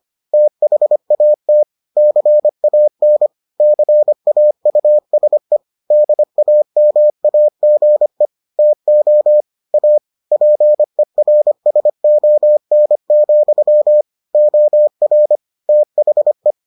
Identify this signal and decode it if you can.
Morse code, heard as a keyed tone, about 25 wpm, 600 Hz. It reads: THAT CAN CAUSE DAMAGE TO A PERSON, OR THE